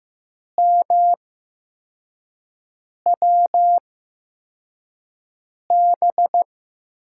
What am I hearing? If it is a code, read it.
Morse code, heard as a keyed tone, about 15 wpm, spaced as for 4 wpm, 700 Hz. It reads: MWB
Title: USA Call Signs - 5 Characters - 8x Character Spacing - 15wpm